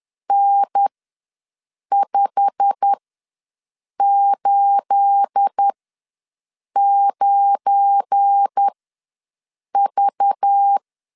Morse code, heard as a keyed tone, about 10 wpm, spaced as for 6 wpm, 800 Hz. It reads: N589V
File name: LEZIONI / 9 28 ws 10wpm 4124